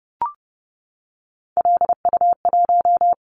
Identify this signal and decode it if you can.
Morse code, heard as a keyed tone, about 30 wpm, 700 Hz. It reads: LU1
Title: Callsign Prefixes for Top 100 DXCC - 30wpm